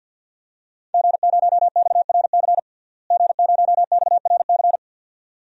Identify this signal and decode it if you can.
Morse code, heard as a keyed tone, about 46 wpm, 700 Hz. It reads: G0XRC G0XRC
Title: DXCC-100 Call Signs - 5 Characters - Repeated 2x - 50wpm